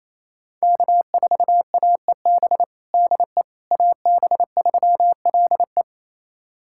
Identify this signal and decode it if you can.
Morse code, heard as a keyed tone, about 27 wpm, 700 Hz. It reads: K4AEB DE AB3LE
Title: USA Callsign DE Callsign - 5 Characters - 28wpm